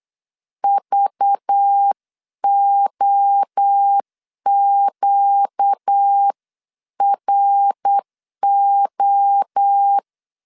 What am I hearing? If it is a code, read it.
Morse code, heard as a keyed tone, about 8 wpm, 800 Hz. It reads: VOQRO
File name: LEZIONI / 6 PKX0 / 6 PKX 8wpm 3041